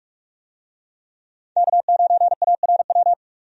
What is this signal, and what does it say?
Morse code, heard as a keyed tone, about 46 wpm, 700 Hz. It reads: K9ARW
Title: USA Call Signs - 5 Characters - 45wpm